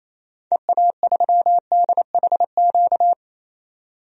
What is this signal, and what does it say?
Morse code, heard as a keyed tone, about 28 wpm, 700 Hz. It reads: EA3DHQ
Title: DXCC-100 Call Signs - 6 Characters - 28wpm